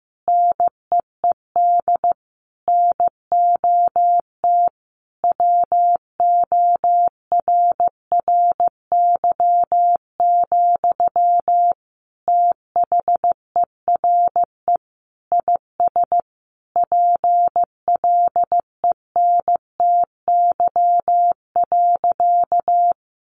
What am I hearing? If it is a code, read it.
Morse code, heard as a keyed tone, about 15 wpm, 700 Hz. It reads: NEED NOT WORRY, THERE IS PLENTY.
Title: Sentences from Top 1300 Words 15wpm